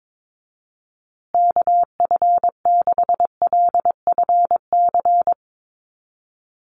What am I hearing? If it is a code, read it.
Morse code, heard as a keyed tone, about 22 wpm, 700 Hz. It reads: KF6LFC
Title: USA Call Signs Repeated 3x 22wpm